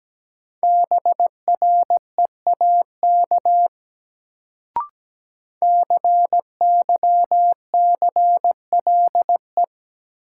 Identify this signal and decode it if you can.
Morse code, heard as a keyed tone, about 17 wpm, 700 Hz. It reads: BREAK CYCLE
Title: Sentences from Top 900 Words - New Word Drill - 17wpm